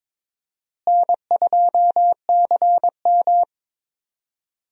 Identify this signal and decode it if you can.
Morse code, heard as a keyed tone, about 21 wpm, 700 Hz. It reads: N2CM
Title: USA Call Signs - 4 Characters - Repeated 3x - 22wpm